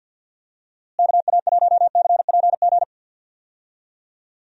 Morse code, heard as a keyed tone, about 51 wpm, 700 Hz. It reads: KA1CPG
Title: USA Call Signs - 6 Characters - Repeated 2x - 50wpm